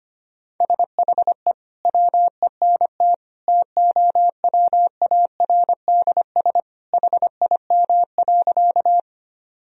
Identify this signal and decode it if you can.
Morse code, heard as a keyed tone, about 25 wpm, 700 Hz. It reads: SHE WENT TOWARDS HIM.
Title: Sentences from Top 1000 Words - Review Part 2 - 25wpm